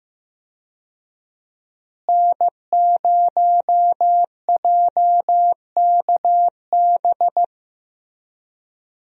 Morse code, heard as a keyed tone, about 15 wpm, 700 Hz. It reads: N0JKB
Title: USA Call Signs - 5 Characters - Repeated 2x - 15wpm